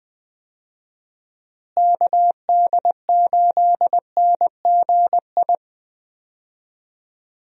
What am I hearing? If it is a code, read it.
Morse code, heard as a keyed tone, about 20 wpm, 700 Hz. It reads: KD8NGI
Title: USA Call Signs - 6 Characters - Repeated 2x - 20wpm